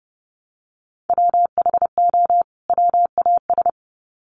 Morse code, heard as a keyed tone, about 30 wpm, 700 Hz. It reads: WHO WAS
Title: Sets of 2 Words 30wpm